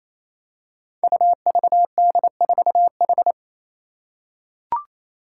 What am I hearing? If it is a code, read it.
Morse code, heard as a keyed tone, about 28 wpm, 700 Hz. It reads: UVD4H
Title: Five Challenging Letter-Number Combinations 28wpm